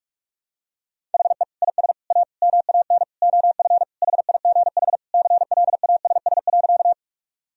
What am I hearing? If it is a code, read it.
Morse code, heard as a keyed tone, about 46 wpm, 700 Hz. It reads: HE IS A MAN OF HIGH CLASS.
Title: Sentences from Top 1000 Words - Review Part 2 - 45wpm